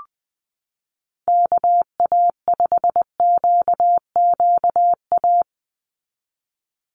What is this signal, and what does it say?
Morse code, heard as a keyed tone, about 20 wpm, 700 Hz. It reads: KA5QQA
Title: NA SOTA Chaser Callsigns 20wpm